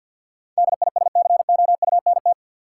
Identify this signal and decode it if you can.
Morse code, heard as a keyed tone, about 51 wpm, 700 Hz. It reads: DISCOUNT
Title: Sentences from Top 1200 Words - New Word Drill - 50wpm